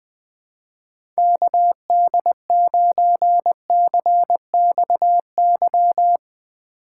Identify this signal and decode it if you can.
Morse code, heard as a keyed tone, about 20 wpm, 700 Hz. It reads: KD9CXY
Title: USA Call Signs - 6 Characters - 3x Character Spacing - 20wpm